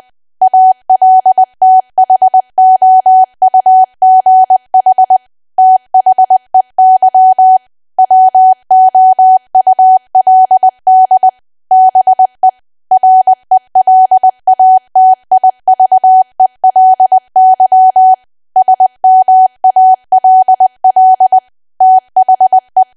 Morse code, heard as a keyed tone, about 20 wpm, 750 Hz. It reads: ALTHOUGH THEY WOULD BE RELATIVELY SMALL THE
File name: cwa / W1AW-Level3 / 20WPM